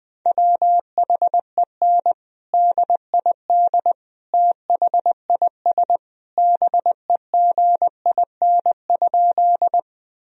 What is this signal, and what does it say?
Morse code, heard as a keyed tone, about 20 wpm, 700 Hz. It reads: WHEN DID THIS BEGIN?
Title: Sentences from Top 1000 Words - Review Part 1 - 20wpm